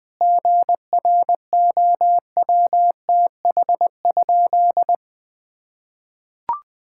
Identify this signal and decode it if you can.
Morse code, heard as a keyed tone, about 19 wpm, 700 Hz. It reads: GROWTH?
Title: Sentences from Top 1000 Words - Review Part 5 - 20wpm